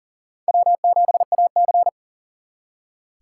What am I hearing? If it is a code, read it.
Morse code, heard as a keyed tone, about 47 wpm, 700 Hz. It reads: W7AC
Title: Speed Racing - USA 1x2 Call Signs 40wpm